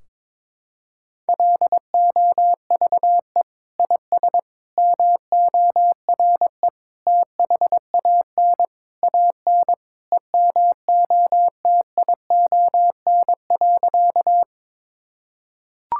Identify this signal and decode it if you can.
Morse code, heard as a keyed tone, about 22 wpm, 700 Hz. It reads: LOVE IS MORE THAN AN EMOTION.
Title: Sentences from Top 1600 Words 22wpm